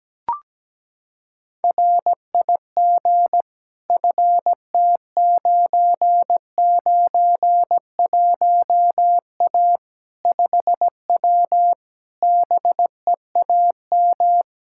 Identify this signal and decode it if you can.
Morse code, heard as a keyed tone, about 17 wpm, 700 Hz. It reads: RIG FT991A 5W BEAM
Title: Practice QSO - Exchange 2 - Info - 17wpm